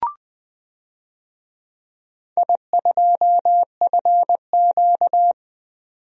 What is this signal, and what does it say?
Morse code, heard as a keyed tone, about 20 wpm, 700 Hz. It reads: I2FQ
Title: DXCC-100 Call Signs - 4 Characters - 20wpm